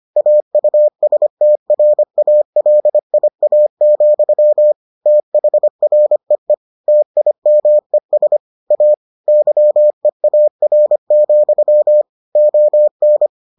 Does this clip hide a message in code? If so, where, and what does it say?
Yes: Morse code from 0.2 to 13.3 s, heard as a keyed tone, about 25 wpm, 600 Hz. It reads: AUSTRALIA, THREE TIMES A YEAR, ON